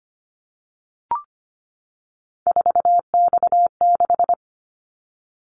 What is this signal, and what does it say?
Morse code, heard as a keyed tone, about 25 wpm, 700 Hz. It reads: 4X6
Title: Callsign Prefixes for Top 50 DXCC - 25wpm